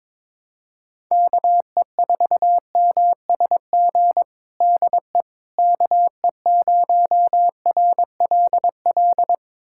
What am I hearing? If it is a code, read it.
Morse code, heard as a keyed tone, about 22 wpm, 700 Hz. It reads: KE4MSG DE KE0RLL
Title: USA Callsign DE Callsign - 6 Characters - 22wpm